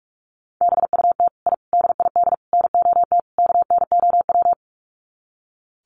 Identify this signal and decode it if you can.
Morse code, heard as a keyed tone, about 44 wpm, 700 Hz. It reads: BUT I DID NOT KNOW
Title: Sets of 5 Words Encore 45wpm